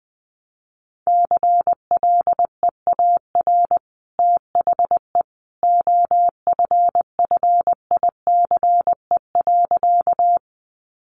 Morse code, heard as a keyed tone, about 20 wpm, 700 Hz. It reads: CLEAR THE OFFICE.
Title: Sentences from Top 1000 Words 20wpm